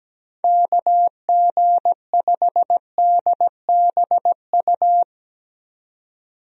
Morse code, heard as a keyed tone, about 17 wpm, 700 Hz. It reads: KG5DBU
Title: USA Call Signs - 6 Characters - 17wpm